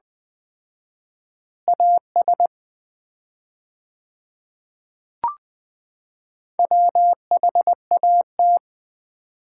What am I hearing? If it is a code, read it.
Morse code, heard as a keyed tone, about 20 wpm, 700 Hz. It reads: AS WHAT
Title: 4x Word-Spacing - Sets of 2 Words - 20wpm